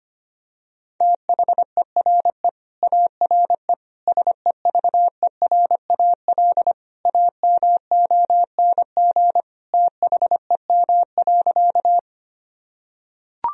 Morse code, heard as a keyed tone, about 25 wpm, 700 Hz. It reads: THERE ARE SEVERAL AMONG THEM.
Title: Sentences from Top 1000 Words - Review Part 2 - 25wpm